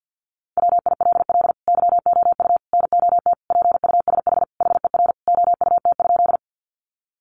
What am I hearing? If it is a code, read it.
Morse code, heard as a keyed tone, about 50 wpm, 700 Hz. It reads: WILL YOU NOT PUSH HER OUT?
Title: Sentences from Top 1300 Words 50wpm